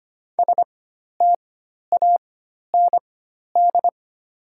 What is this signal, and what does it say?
Morse code, heard as a keyed tone, about 25 wpm, spaced as for 12 wpm, 700 Hz. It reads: STAND
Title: 4x Character Spacing - Top 500 Words - 25wpm